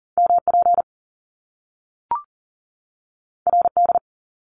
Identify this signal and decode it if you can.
Morse code, heard as a keyed tone, about 39 wpm, 700 Hz. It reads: MP RD
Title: Top Two Letter Combinations in Words 40wpm